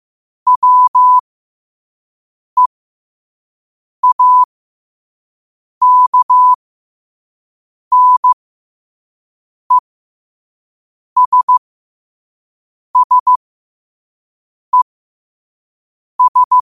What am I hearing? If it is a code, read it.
Morse code, heard as a keyed tone, about 15 wpm, spaced as for 5 wpm, 1000 Hz. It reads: WEAKNESSES
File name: qotdfiles / 5wpm